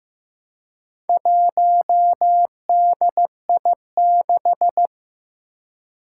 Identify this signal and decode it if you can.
Morse code, heard as a keyed tone, about 15 wpm, 700 Hz. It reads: 1DI6
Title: Four Challenging Letter-Number Combinations 15wpm